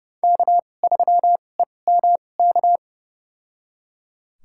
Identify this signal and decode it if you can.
Morse code, heard as a keyed tone, about 30 wpm, spaced as for 22 wpm, 700 Hz. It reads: K3EMK
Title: USA Call Signs - 5 Characters - 2x Character Spacing - 30wpm